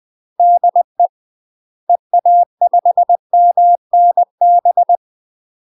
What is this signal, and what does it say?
Morse code, heard as a keyed tone, about 20 wpm, 700 Hz. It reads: DE EA5MNB
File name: UK MASTER QSOs TRAINING / 20WPM / UK MASTER QSOs 0060